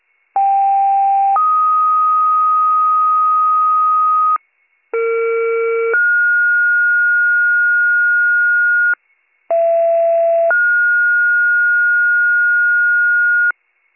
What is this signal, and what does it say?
Two-tone page, heard as a one-second tone, then a three-second tone, about 790 then 1250 Hz, about 460 then 1430 Hz, about 670 then 1430 Hz, with noise. It is heard from 0.4 to 13.5 s.